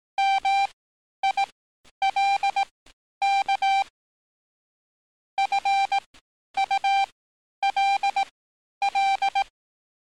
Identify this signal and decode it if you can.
Morse code, heard as a keyed tone, about 18 wpm, spaced as for 10 wpm, 790 Hz. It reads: MILK FULL
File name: training / NAVPERS 11049to11060 / NAVPERS 11050 RA RB